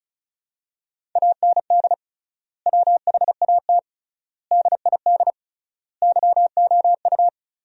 Morse code, heard as a keyed tone, about 36 wpm, 700 Hz. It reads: AND WHAT DID YOU
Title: 3x Word-Spacing - Sentences from Top 100 Words - 35wpm